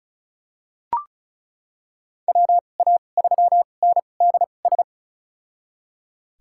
Morse code, heard as a keyed tone, about 36 wpm, spaced as for 25 wpm, 700 Hz. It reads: WA3NDS
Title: USA Call Signs - 6 Characters - 2x Character Spacing - 35wpm